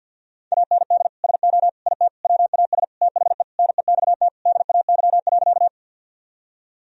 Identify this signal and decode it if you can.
Morse code, heard as a keyed tone, about 48 wpm, 700 Hz. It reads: AND SO IT WAS THE NEXT DAY.
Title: Sentences from Top 1000 Words - Review Part 1 - 50wpm